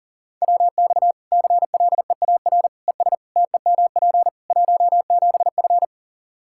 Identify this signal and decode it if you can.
Morse code, heard as a keyed tone, about 40 wpm, 700 Hz. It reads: WX CLEAR ES TEMP 17F
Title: Practice QSO - Exchange 2 - Info - 40wpm